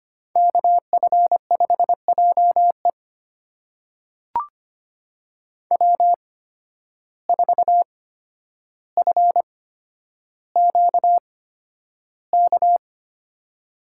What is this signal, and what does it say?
Morse code, heard as a keyed tone, about 25 wpm, spaced as for 11 wpm, 700 Hz. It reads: KF5JE W4FQK
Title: USA Call Signs - 5 Characters - 8x Character Spacing - 25wpm